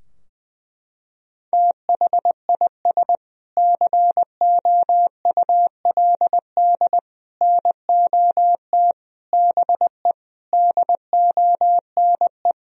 Morse code, heard as a keyed tone, about 20 wpm, 700 Hz. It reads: THIS COULD NOT BE DONE
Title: Sentences from Top 1000 Words - Review Part 3 - 20wpm